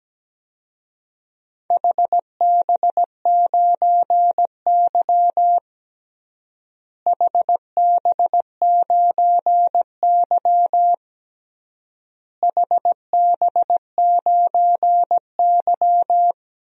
Morse code, heard as a keyed tone, about 17 wpm, 700 Hz. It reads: HB9Y HB9Y HB9Y
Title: DXCC-100 Call Signs - 4 Characters - Repeated 3x - 17wpm